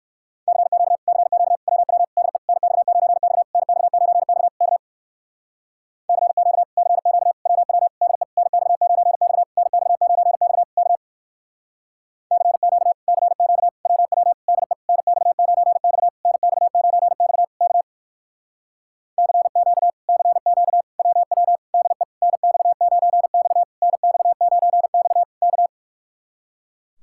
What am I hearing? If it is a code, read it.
Morse code, heard as a keyed tone, about 57 wpm, 700 Hz. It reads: CQ CQ WW DE NX9X NX9X K CQ CQ WW DE NX9X NX9X K CQ CQ WW DE NX9X NX9X K CQ CQ WW DE NX9X NX9X K